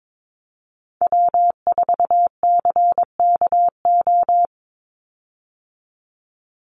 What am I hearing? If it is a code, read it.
Morse code, heard as a keyed tone, about 22 wpm, 700 Hz. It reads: W4CKO
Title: USA Call Signs 22wpm